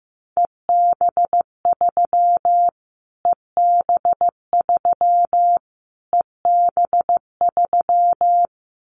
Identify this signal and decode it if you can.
Morse code, heard as a keyed tone, about 15 wpm, 700 Hz. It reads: EB3 EB3 EB3
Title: Callsign Prefixes for Top 25 DXCC - Repeated 3x - 15wpm